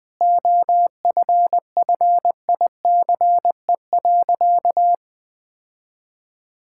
Morse code, heard as a keyed tone, about 20 wpm, 700 Hz. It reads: OFFICE.
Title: Sentences with 3-Word Phrases - 20wpm